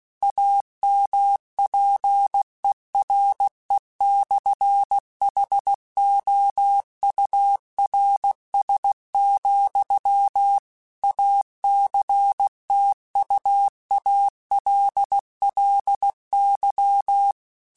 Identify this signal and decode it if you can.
Morse code, heard as a keyed tone, about 16 wpm, 800 Hz. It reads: AMPERE/HOURS, ACTUALLY
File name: Technical Subjects / Technical Subjects-mp3s / Sw Pwr Sup @15